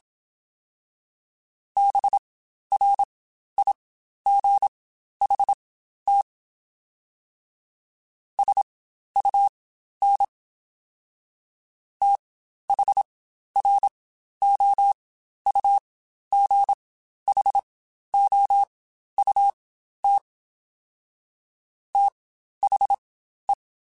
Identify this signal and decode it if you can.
Morse code, heard as a keyed tone, about 27 wpm, spaced as for 12 wpm, 800 Hz. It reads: BRIGHT SUN THROUGHOUT THE
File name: Spaced Characters / Spaced Characters-mp3s / Weather 5 12 sp @25